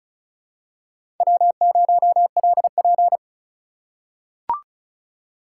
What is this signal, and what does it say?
Morse code, heard as a keyed tone, about 36 wpm, 700 Hz. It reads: W0LP
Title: USA Call Signs - 4 Characters - 4x Character Spacing - 35wpm